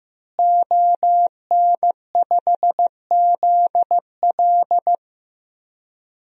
Morse code, heard as a keyed tone, about 15 wpm, 700 Hz. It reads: ON5ZL
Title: DXCC-100 Call Signs - 5 Characters - Repeated 2x - 15wpm